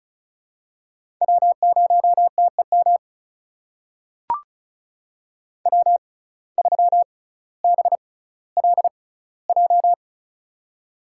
Morse code, heard as a keyed tone, about 36 wpm, spaced as for 18 wpm, 700 Hz. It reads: W0TEM W3BLJ
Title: USA Call Signs - 5 Characters - 6x Character Spacing - 35wpm